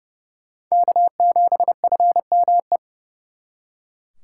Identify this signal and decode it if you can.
Morse code, heard as a keyed tone, about 36 wpm, 700 Hz. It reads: K7FME
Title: Speed Racing - USA Call Signs 30wpm